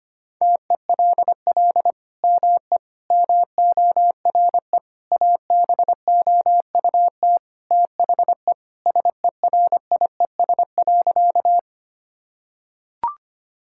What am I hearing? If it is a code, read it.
Morse code, heard as a keyed tone, about 25 wpm, 700 Hz. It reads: TELL ME MORE ABOUT THE SERIES.
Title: Sentences from Top 1000 Words - Review Part 5 - 25wpm